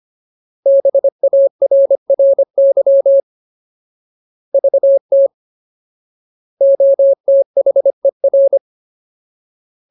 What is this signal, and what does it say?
Morse code, heard as a keyed tone, about 25 wpm, 550 Hz. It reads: BARRY VT OTHER
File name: Ultimate Word List / 25WPM Ultimate